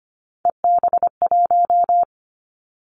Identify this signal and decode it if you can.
Morse code, heard as a keyed tone, about 25 wpm, 700 Hz. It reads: EB1
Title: Callsign Prefixes for Top 25 DXCC - Repeated 3x - 25wpm